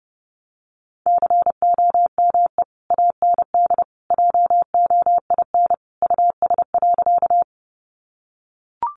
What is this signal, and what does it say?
Morse code, heard as a keyed tone, about 30 wpm, 700 Hz. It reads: COME AND JOIN US.
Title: Sentences from Top 1400 Words 30wpm